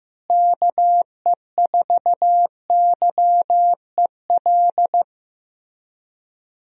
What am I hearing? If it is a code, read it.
Morse code, heard as a keyed tone, about 15 wpm, 700 Hz. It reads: KE4YEL
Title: USA Call Signs - 6 Characters - Repeated 3x - 15wpm